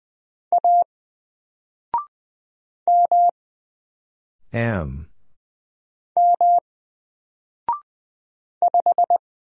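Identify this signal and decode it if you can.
Morse code, heard as a keyed tone, about 20 wpm, 700 Hz. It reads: A M M 5
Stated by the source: Single letter and number Morse drill at 20 wpm, from N4LCD.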